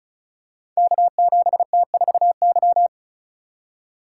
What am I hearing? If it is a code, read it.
Morse code, heard as a keyed tone, about 36 wpm, 700 Hz. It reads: K7T4Y
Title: Code Groups - 5 Letter:Numbers - 35wpm